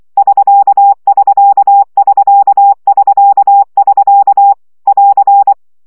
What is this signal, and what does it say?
Morse code, heard as a keyed tone, about 24 wpm, 800 Hz. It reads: <SK><SK><SK><SK><SK> <AR>
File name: CW gyakorlas sk 20 baud